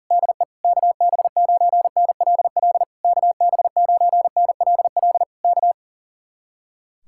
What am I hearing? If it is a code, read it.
Morse code, heard as a keyed tone, about 38 wpm, 700 Hz. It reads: DE KB9NLL KB9NLL K
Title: Speed-Racing Common QSO Phrases Encore - 40wpm